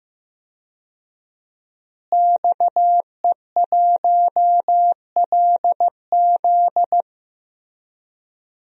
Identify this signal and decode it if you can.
Morse code, heard as a keyed tone, about 15 wpm, 700 Hz. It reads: XE1LZ
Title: DXCC-100 Call Signs - 5 Characters - Repeated 2x - 15wpm